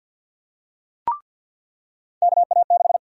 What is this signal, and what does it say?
Morse code, heard as a keyed tone, about 48 wpm, 700 Hz. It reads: KA6
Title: USA Call Sign Prefixes - 50wpm